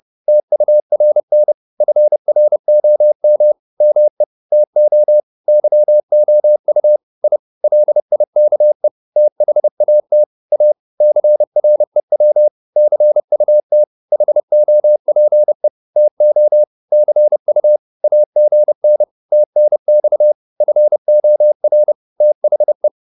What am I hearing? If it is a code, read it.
Morse code, heard as a keyed tone, about 30 wpm, 600 Hz. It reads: TURN FROM ME TO YOU I LIKE THAT A CREW CUT HOPE TO CU AGN TNX FOR THE